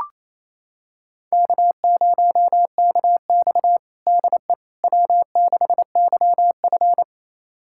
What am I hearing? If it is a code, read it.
Morse code, heard as a keyed tone, about 28 wpm, 700 Hz. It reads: K0KX DE W6YF
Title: USA Callsign DE Callsign - 4 Characters - 28wpm